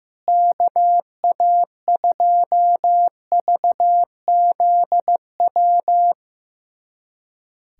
Morse code, heard as a keyed tone, about 15 wpm, 700 Hz. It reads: KA2VZW
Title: USA Callsign DE Callsign 15wpm